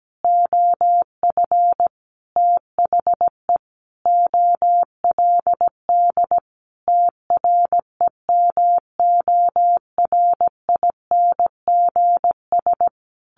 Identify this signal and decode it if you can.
Morse code, heard as a keyed tone, about 17 wpm, 700 Hz. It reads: OF THE OLD TREMORINGS
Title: A Sound Of Thunder 17wpm